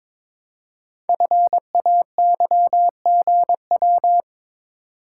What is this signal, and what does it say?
Morse code, heard as a keyed tone, about 22 wpm, 700 Hz. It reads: FAYGW
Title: Code Groups - 5 Letters - 22wpm